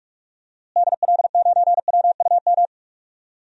Morse code, heard as a keyed tone, about 46 wpm, 700 Hz. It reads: DL9WUM
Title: DXCC-100 Call Signs - 6 Characters - Repeated 2x - 45wpm